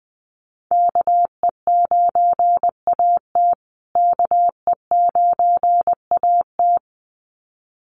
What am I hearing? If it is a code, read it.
Morse code, heard as a keyed tone, about 20 wpm, 700 Hz. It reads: KE9AT KE9AT
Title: USA Call Signs Repeated 2x 20wpm